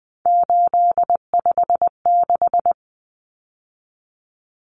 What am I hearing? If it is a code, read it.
Morse code, heard as a keyed tone, about 20 wpm, 700 Hz. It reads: 856
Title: Numbers - 3-Digit 20wpm